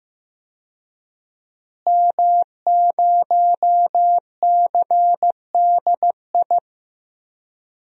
Morse code, heard as a keyed tone, about 15 wpm, 700 Hz. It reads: M0CDI
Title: DXCC-100 Call Signs - 5 Characters - Repeated 3x - 15wpm